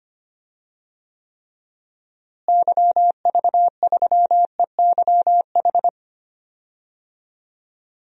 Morse code, heard as a keyed tone, about 25 wpm, 700 Hz. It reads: YV3EYH
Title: DXCC-100 Call Signs - 6 Characters - Repeated 2x - 25wpm